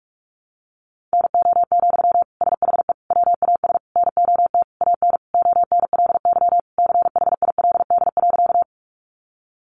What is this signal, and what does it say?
Morse code, heard as a keyed tone, about 44 wpm, 700 Hz. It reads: NO, SHE WAS NOT AN ONLY CHILD.
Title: Sentences from Top 800 Words 45wpm